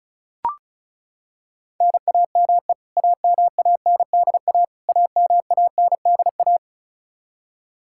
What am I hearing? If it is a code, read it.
Morse code, heard as a keyed tone, about 34 wpm, 700 Hz. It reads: NAME AMANDA AMANDA
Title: K1USN SST QSO Practice 35wpm